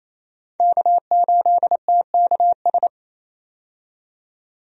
Morse code, heard as a keyed tone, about 28 wpm, 700 Hz. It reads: K8TKS
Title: USA Call Signs - 5 Characters - 28wpm